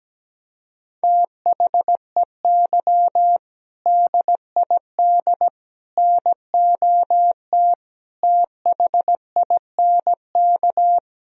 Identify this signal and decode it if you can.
Morse code, heard as a keyed tone, about 17 wpm, 700 Hz. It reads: THEY DID NOT THINK
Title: Sentences from Top 1000 Words - Review Part 3 - 17wpm